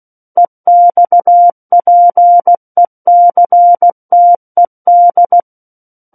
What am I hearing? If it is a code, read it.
Morse code, heard as a keyed tone, about 16 wpm, 700 Hz. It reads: EXPECTED
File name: Book One / 07-Chapter 7 - 16 WPM